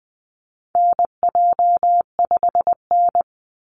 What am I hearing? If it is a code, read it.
Morse code, heard as a keyed tone, about 20 wpm, 700 Hz. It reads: NJ5N
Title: Call Signs Repeated 3x - 20wpm